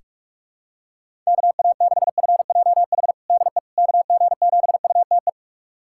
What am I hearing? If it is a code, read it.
Morse code, heard as a keyed tone, about 41 wpm, 700 Hz. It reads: KA6FJH DE KG7UTE
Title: USA Callsign DE Callsign - 6 Characters - 45wpm